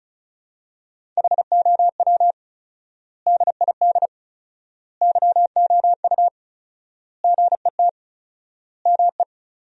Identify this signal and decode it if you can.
Morse code, heard as a keyed tone, about 36 wpm, 700 Hz. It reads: HOW DID YOU GET ME
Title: 4x Word-Spacing - Sentences from Top 200 Words - 35wpm